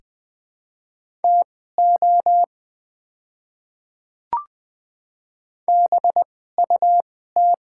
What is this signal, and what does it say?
Morse code, heard as a keyed tone, about 21 wpm, spaced as for 15 wpm, 700 Hz. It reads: TO BUT
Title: 2x Character Spacing - Top 200 Words - 20wpm